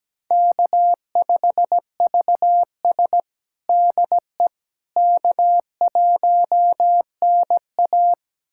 Morse code, heard as a keyed tone, about 17 wpm, 700 Hz. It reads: K5VS DE K1NA
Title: USA Callsign DE Callsign - 4 Characters - 17wpm